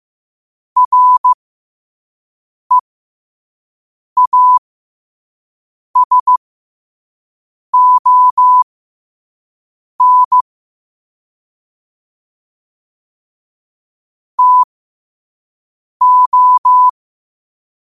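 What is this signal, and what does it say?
Morse code, heard as a keyed tone, about 15 wpm, spaced as for 5 wpm, 1000 Hz. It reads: REASON TO
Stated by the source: Morse code quote practice at 5 words per minute.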